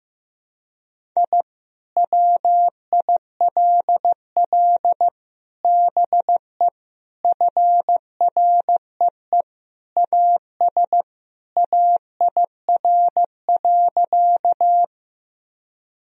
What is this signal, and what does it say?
Morse code, heard as a keyed tone, about 15 wpm, 700 Hz. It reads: I WILL BE FREE AS AIR.
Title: Sentences from Top 1000 Words - Review Part 1 - 15wpm